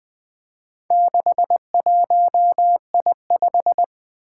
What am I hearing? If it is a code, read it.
Morse code, heard as a keyed tone, about 20 wpm, 700 Hz. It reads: 61I5
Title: Four Challenging Letter-Number Combinations 20wpm